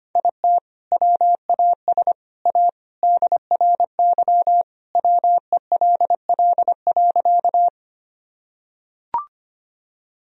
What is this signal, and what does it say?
Morse code, heard as a keyed tone, about 25 wpm, 700 Hz. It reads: IT WAS A DRY WELL.
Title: Sentences from Top 1000 Words - Review Part 3 - 25wpm